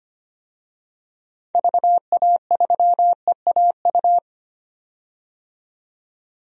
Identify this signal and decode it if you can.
Morse code, heard as a keyed tone, about 25 wpm, 700 Hz. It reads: VA3EAU
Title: DXCC-100 Call Signs - 6 Characters - 25wpm